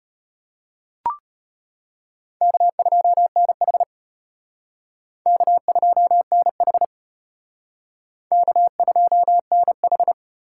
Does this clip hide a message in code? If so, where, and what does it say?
Yes: Morse code from 2.4 to 10.1 s, heard as a keyed tone, about 34 wpm, 700 Hz. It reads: K2NH K2NH K2NH